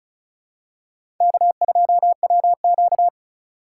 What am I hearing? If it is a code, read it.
Morse code, heard as a keyed tone, about 34 wpm, 700 Hz. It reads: K2WQ
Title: USA Call Signs - 2x Character Spacing - 35wpm